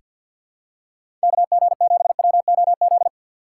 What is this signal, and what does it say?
Morse code, heard as a keyed tone, about 46 wpm, 700 Hz. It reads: KG7WOZ
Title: USA Call Signs - 2x Character Spacing - 50wpm